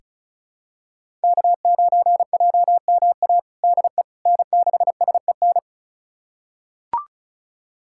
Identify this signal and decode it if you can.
Morse code, heard as a keyed tone, about 36 wpm, 700 Hz. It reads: K9JMA DE N6SEN
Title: USA Callsign DE Callsign - 5 Characters - 35wpm